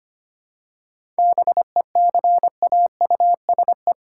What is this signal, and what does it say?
Morse code, heard as a keyed tone, about 25 wpm, 700 Hz. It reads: BECAUSE
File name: practice / 25wpm / All-Age Appropriate Jokes - 25wpm